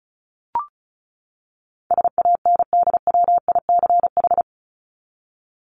Morse code, heard as a keyed tone, about 34 wpm, 700 Hz. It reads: SANDWICH
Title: Top 1751-2000 Words 35wpm